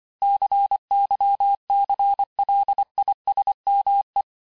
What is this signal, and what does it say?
Morse code, heard as a keyed tone, about 25 wpm, 800 Hz. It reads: CYCLISME
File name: cours-du-lundi files / cours-1589-texte-clair-vitesse-18mpm-puis-23mpm